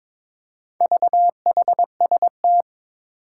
Morse code, heard as a keyed tone, about 22 wpm, 700 Hz. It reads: VHST
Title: Four Challenging Letter-Number Combinations 22wpm